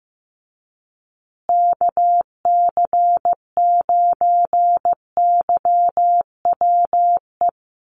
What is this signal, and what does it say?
Morse code, heard as a keyed tone, about 15 wpm, 700 Hz. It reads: KC9YWE
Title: USA Call Signs Repeated 3x 15wpm